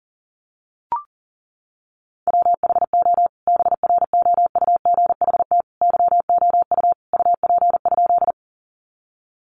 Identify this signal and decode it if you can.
Morse code, heard as a keyed tone, about 40 wpm, 700 Hz. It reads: WHO BROUGHT YOU UP?
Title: Sentences from Top 500 Words 40wpm